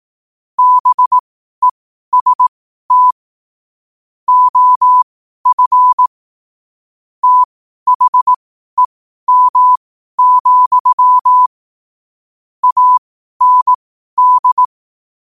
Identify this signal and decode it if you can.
Morse code, heard as a keyed tone, about 18 wpm, spaced as for 12 wpm, 1000 Hz. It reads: BEST OF THEM, AND
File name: qotdfiles / 12wpm